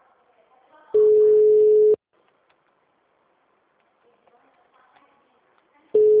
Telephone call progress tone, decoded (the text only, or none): ringback tone